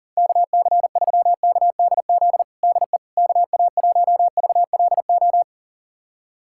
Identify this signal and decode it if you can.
Morse code, heard as a keyed tone, about 37 wpm, 700 Hz. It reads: KC3KDZ DE KA1VLO
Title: USA Callsign DE Callsign - 6 Characters - 40wpm